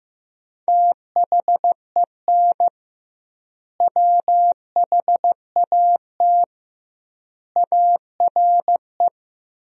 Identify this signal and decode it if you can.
Morse code, heard as a keyed tone, about 15 wpm, 700 Hz. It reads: THEN WHAT ARE
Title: 2x Word-Spacing - Sentences from Top 100 Words - 15wpm